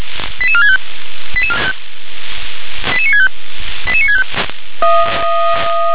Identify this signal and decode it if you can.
Five-tone selective call (ZVEI), heard as quick runs of five tones, tones of about 70 ms. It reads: 88465 88965 88965